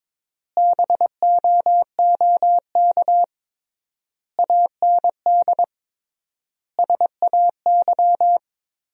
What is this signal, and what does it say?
Morse code, heard as a keyed tone, about 22 wpm, 700 Hz. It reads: BOOK AND SAY
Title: 3x Word-Spacing - Sentences from Top 400 Words - 22wpm